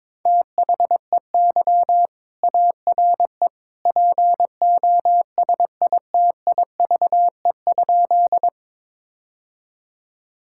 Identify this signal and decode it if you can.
Morse code, heard as a keyed tone, about 22 wpm, 700 Hz. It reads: THEY ARE POSITIVE?
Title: Sentences from Top 1000 Words - Review Part 5 - 22wpm